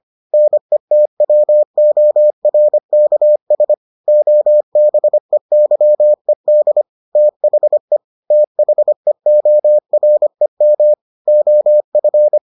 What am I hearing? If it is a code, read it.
Morse code, heard as a keyed tone, about 25 wpm, 600 Hz. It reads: NETWORKS OBEYED THE THEOREM OF